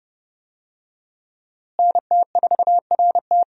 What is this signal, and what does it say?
Morse code, heard as a keyed tone, about 30 wpm, 700 Hz. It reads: NT4RT
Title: USA Call Signs - 5 Characters - Repeated 3x - 30wpm